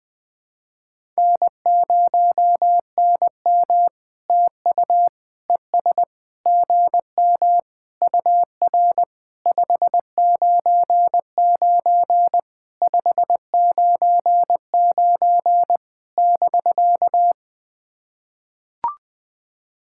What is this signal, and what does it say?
Morse code, heard as a keyed tone, about 20 wpm, 700 Hz. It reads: N0NM TU ES GM UR 599 599 <BK>
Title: POTA QSO Practice - 20wpm